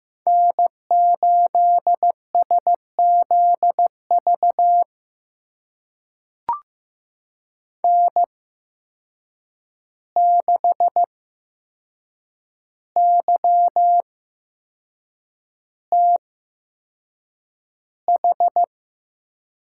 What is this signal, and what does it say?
Morse code, heard as a keyed tone, about 16 wpm, spaced as for 6 wpm, 700 Hz. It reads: N8SZV N6YTH
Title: USA Call Signs - 5 Characters - 8x Character Spacing - 15wpm